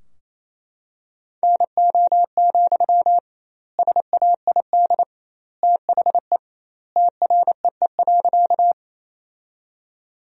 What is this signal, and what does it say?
Morse code, heard as a keyed tone, about 28 wpm, 700 Hz. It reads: NO, SAID THE TREE.
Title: 2x Word-Spacing - Sentences from Top 300 Words - 28wpm